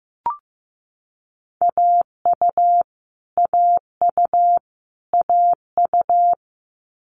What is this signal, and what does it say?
Morse code, heard as a keyed tone, about 15 wpm, 700 Hz. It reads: AU AU AU
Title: Multi-letter Phonemes Repeated 3x 15wpm